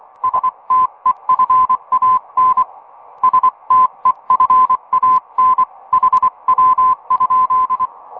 Morse code, heard as a keyed tone, about 24 wpm, 1020 Hz. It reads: STEFAN STEFAN HW?
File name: attachments